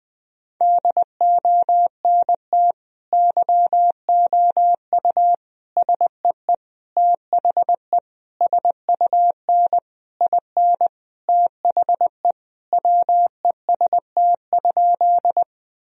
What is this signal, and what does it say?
Morse code, heard as a keyed tone, about 21 wpm, 700 Hz. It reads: DONT YOU SEE THE SUN IN THE WEST?
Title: Sentences from Top 1000 Words - Review Part 2 - 20wpm